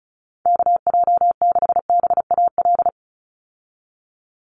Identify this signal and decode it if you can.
Morse code, heard as a keyed tone, about 36 wpm, 700 Hz. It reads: KJ6BAL
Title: USA Call Signs 35wpm